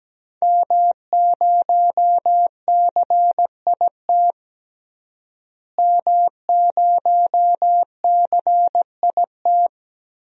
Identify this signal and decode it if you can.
Morse code, heard as a keyed tone, about 17 wpm, 700 Hz. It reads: M0CIT M0CIT
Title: DXCC-100 Call Signs - 5 Characters - Repeated 3x - 17wpm